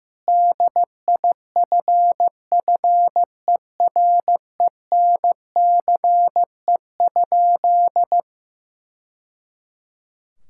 Morse code, heard as a keyed tone, about 15 wpm, 700 Hz. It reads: DIFFERENCE?
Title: Sentences from Top 1000 Words - Review Part 5 - 15wpm